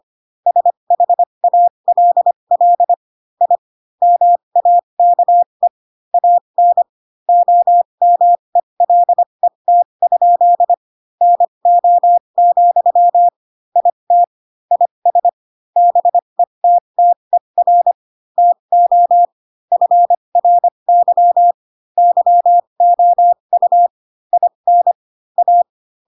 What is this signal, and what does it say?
Morse code, heard as a keyed tone, about 25 wpm, spaced as for 22 wpm, 700 Hz. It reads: SHALL I MAKE AN OMELET? NO, IT IS BETTER TO FRY YOU IN A